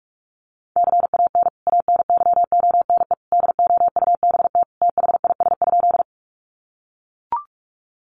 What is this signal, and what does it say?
Morse code, heard as a keyed tone, about 44 wpm, 700 Hz. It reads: CAN ANYONE DOUBT THIS?
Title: Sentences from Top 1500 Words 45wpm